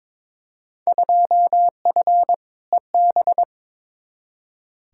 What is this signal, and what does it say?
Morse code, heard as a keyed tone, about 22 wpm, 700 Hz. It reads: 2F EB
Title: Field Day Exchanges Encore - 22wpm